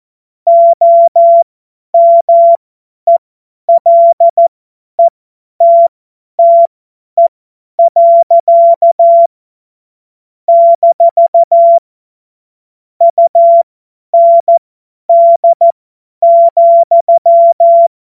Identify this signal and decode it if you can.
Morse code, heard as a keyed tone, about 14 wpm, spaced as for 10 wpm, 680 Hz. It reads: OMELETTE. - UND,
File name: Morsekurs / Witze / Witz 26 März 2024 50 F70